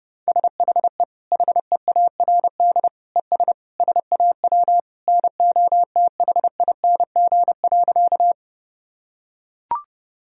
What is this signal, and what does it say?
Morse code, heard as a keyed tone, about 30 wpm, 700 Hz. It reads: SHE HEARD ES SAW NOTHING.
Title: Sentences from Top 500 Words and Abbreviations 30wpm